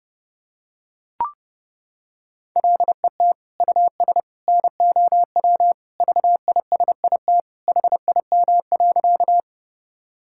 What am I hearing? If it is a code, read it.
Morse code, heard as a keyed tone, about 30 wpm, 700 Hz. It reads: LET US NOW VISIT HIM.
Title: Sentences from Top 1000 Words - Review Part 6 - 30wpm